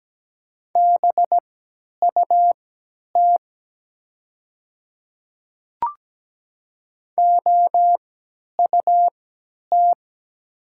Morse code, heard as a keyed tone, about 17 wpm, spaced as for 10 wpm, 700 Hz. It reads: BUT OUT
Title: 3x Character Spacing - Top 500 Words - 17wpm